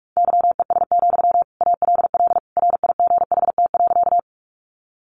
Morse code, heard as a keyed tone, about 44 wpm, 700 Hz. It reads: YES, ALL RIGHT.
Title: Sentences from Top 500 Words 45wpm